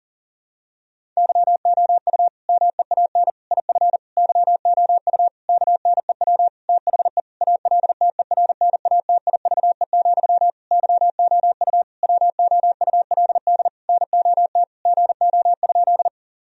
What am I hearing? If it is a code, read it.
Morse code, heard as a keyed tone, about 40 wpm, 700 Hz. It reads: YOU MEAN IF YOU KNEW THE ALTERNATIVE, YOU WOULD NOT GO?
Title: Sentences from Top 1000 Words - Review Part 4 - 40wpm